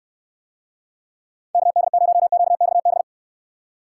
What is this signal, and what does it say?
Morse code, heard as a keyed tone, about 67 wpm, 700 Hz. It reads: KD0QXB